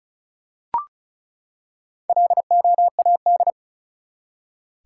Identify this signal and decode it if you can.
Morse code, heard as a keyed tone, about 36 wpm, 700 Hz. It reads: LOAD
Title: Four Letter Words - 35wpm